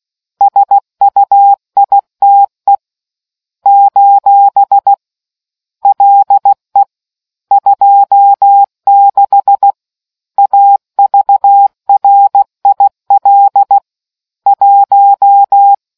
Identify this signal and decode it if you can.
Morse code, heard as a keyed tone, about 16 wpm, 800 Hz. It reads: SUITE : LE 26 AVRIL 1